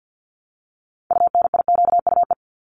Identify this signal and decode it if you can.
Morse code, heard as a keyed tone, about 50 wpm, 700 Hz. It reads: UNIQUE
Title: Multi-letter Phonemes Repeated 3x 50wpm